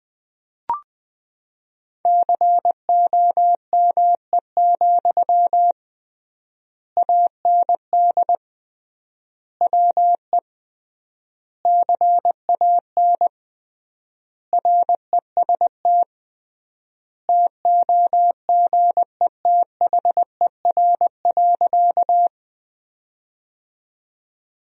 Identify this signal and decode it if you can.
Morse code, heard as a keyed tone, about 20 wpm, 700 Hz. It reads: COME, AND WE CAN REST TOGETHER.
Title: 3x Word-Spacing - Sentences from Top 500 Words - 20wpm